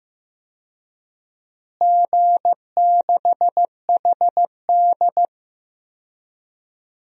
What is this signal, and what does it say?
Morse code, heard as a keyed tone, about 15 wpm, 700 Hz. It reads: G6HD
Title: DXCC-100 Call Signs - 4 Characters - Repeated 2x - 15wpm